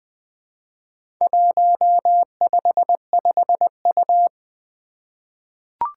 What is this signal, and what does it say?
Morse code, heard as a keyed tone, about 20 wpm, 700 Hz. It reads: 155U
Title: Four Challenging Letter-Number Combinations 20wpm